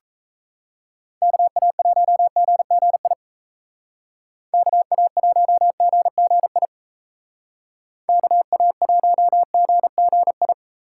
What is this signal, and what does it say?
Morse code, heard as a keyed tone, about 36 wpm, 700 Hz. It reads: KA1GGI KA1GGI KA1GGI